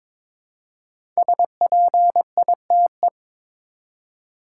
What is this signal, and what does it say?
Morse code, heard as a keyed tone, about 22 wpm, 700 Hz. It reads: SPITE
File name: 5 Letter Words - First 3 Letters Given - 22wpm